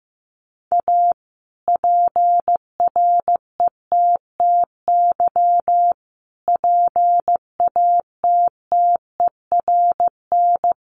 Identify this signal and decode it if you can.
Morse code, heard as a keyed tone, about 15 wpm, 700 Hz. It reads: A PRETTY PATTERN
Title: Sentences from Top 1000 Words 15wpm